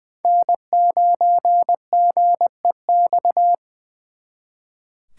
Morse code, heard as a keyed tone, about 23 wpm, 700 Hz. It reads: N9GEX
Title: Speed Racing - USA 1x3 Call Signs 20wpm